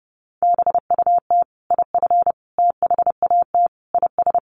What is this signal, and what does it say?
Morse code, heard as a keyed tone, about 30 wpm, 700 Hz. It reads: BUT IF THAT IS
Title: Sentences from Top 200 Words 30wpm